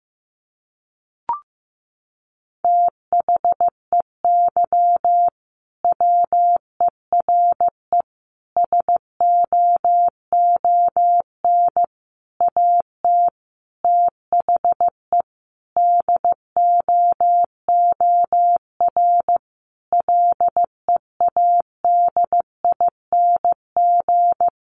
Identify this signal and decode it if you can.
Morse code, heard as a keyed tone, about 15 wpm, 700 Hz. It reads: THEY WERE SOON AT THE DOOR LEADING
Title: Sentences from Top 1200 Words 15wpm